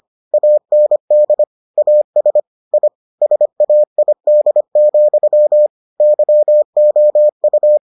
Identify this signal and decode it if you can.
Morse code, heard as a keyed tone, about 25 wpm, 600 Hz. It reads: AND AS I SAID, YOU